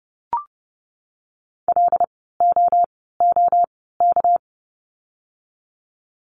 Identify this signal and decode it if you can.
Morse code, heard as a keyed tone, about 30 wpm, spaced as for 17 wpm, 700 Hz. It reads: LOOK
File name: course / 30:10wpm / 237 Lesson 38f - Letter q - Words 30wpm